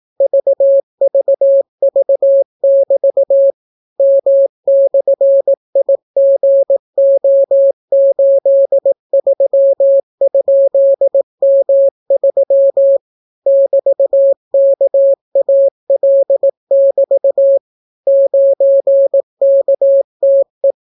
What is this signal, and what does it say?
Morse code, heard as a keyed tone, about 18 wpm, 550 Hz. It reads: VVV<BT> M/IGO83?M3 <BT>KAL<BT> 9KTE